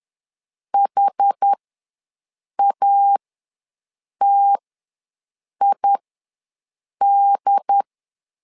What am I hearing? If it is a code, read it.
Morse code, heard as a keyed tone, about 10 wpm, spaced as for 6 wpm, 800 Hz. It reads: HATID
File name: 2 UHDN ws 10wpm